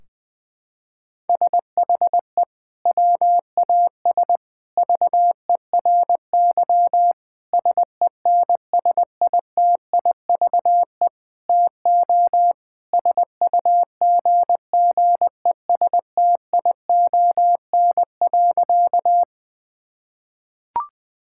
Morse code, heard as a keyed tone, about 20 wpm, 700 Hz. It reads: SHE WAS VERY SENSITIVE TO SUGGESTION.
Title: Sentences from Top 1600 Words 20wpm